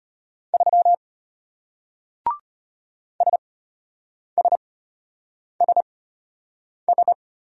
Morse code, heard as a keyed tone, about 33 wpm, spaced as for 8 wpm, 700 Hz. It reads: 3 SSSS